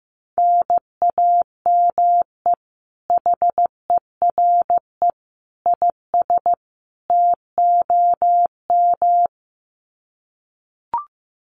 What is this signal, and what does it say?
Morse code, heard as a keyed tone, about 15 wpm, 700 Hz. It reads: NAME HERE IS TOM
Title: Common QSO Phrases 15wpm